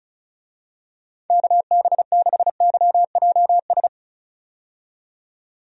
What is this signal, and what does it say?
Morse code, heard as a keyed tone, about 34 wpm, 700 Hz. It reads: KB6YJS
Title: USA Call Signs - 6 Characters - 35wpm